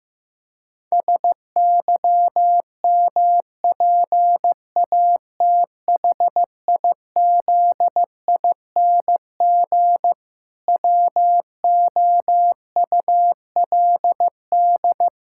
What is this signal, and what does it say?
Morse code, heard as a keyed tone, about 16 wpm, 700 Hz. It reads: SYMPATHIZING WOULD
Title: Pangram Challenge v2 - 15wpm